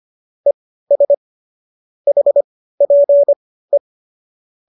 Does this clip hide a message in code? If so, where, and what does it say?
Yes: Morse code from 0.5 to 3.8 s, heard as a keyed tone, about 25 wpm, spaced as for 15 wpm, 570 Hz. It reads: ES HPE